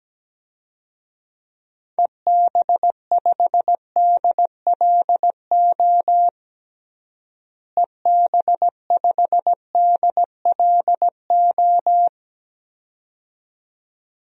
Morse code, heard as a keyed tone, about 17 wpm, 700 Hz. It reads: EB5DLO EB5DLO
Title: DXCC-100 Call Signs - 6 Characters - Repeated 2x - 17wpm